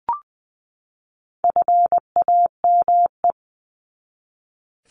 Morse code, heard as a keyed tone, about 20 wpm, 700 Hz. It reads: FAME
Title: 260 Lesson 42e - Words 20-20wpm